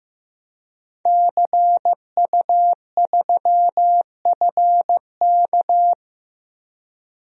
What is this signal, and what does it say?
Morse code, heard as a keyed tone, about 15 wpm, 700 Hz. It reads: CU3FK
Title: DXCC-100 Call Signs - 5 Characters - Repeated 3x - 15wpm